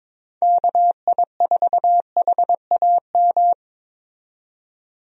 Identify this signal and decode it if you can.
Morse code, heard as a keyed tone, about 22 wpm, 700 Hz. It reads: KI4HAM
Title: USA Call Signs - 6 Characters - 22wpm